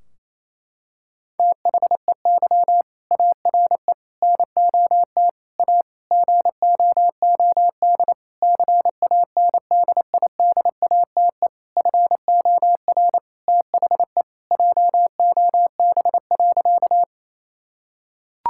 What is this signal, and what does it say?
Morse code, heard as a keyed tone, about 28 wpm, 700 Hz. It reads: THEY ARE NOT A GOOD CANDIDATE FOR THE JOB.
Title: Sentences from Top 1600 Words 28wpm